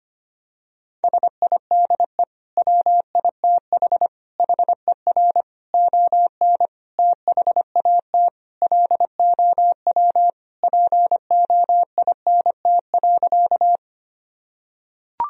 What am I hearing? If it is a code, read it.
Morse code, heard as a keyed tone, about 25 wpm, 700 Hz. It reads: SIDE WITH HER ON THAT LOW POINT.
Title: Sentences from Top 1000 Words - Review Part 1 - 25wpm